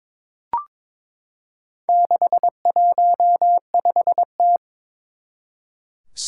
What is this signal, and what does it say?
Morse code, heard as a keyed tone, about 22 wpm, 700 Hz. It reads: 615T
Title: Four Challenging Letter-Number Combinations 22wpm